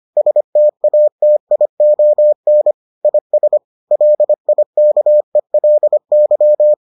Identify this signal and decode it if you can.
Morse code, heard as a keyed tone, about 25 wpm, 600 Hz. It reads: STATION IS LIKELY